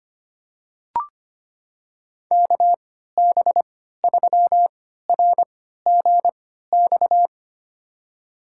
Morse code, heard as a keyed tone, about 25 wpm, spaced as for 14 wpm, 700 Hz. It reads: KB3RGX
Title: USA Call Signs - 6 Characters - 3x Character Spacing - 25wpm